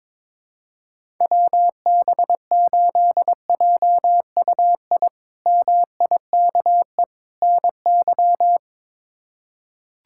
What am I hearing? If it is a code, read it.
Morse code, heard as a keyed tone, about 22 wpm, 700 Hz. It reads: WB8JUI MIKE NY
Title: K1USN SST QSO Practice - 22wpm